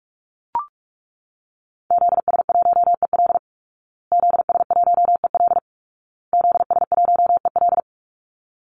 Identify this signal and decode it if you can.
Morse code, heard as a keyed tone, about 44 wpm, 700 Hz. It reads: ZS1EL ZS1EL ZS1EL
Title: Call Signs Repeated 2x - 45wpm